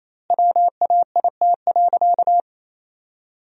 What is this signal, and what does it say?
Morse code, heard as a keyed tone, about 28 wpm, 700 Hz. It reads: WAIT.
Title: 2x Word-Spacing - Sentences from Top 500 Words - 28wpm